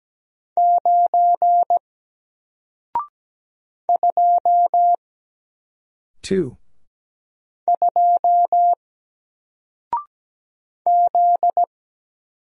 Morse code, heard as a keyed tone, about 17 wpm, 700 Hz. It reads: 9 2 2 Z
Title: Single Letter-Number Alt 17wpm